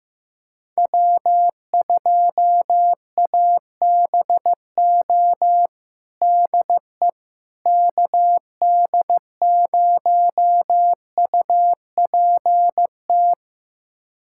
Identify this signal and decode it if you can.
Morse code, heard as a keyed tone, about 15 wpm, 700 Hz. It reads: W2ABO DE KD0UPT
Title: USA Callsign DE Callsign 15wpm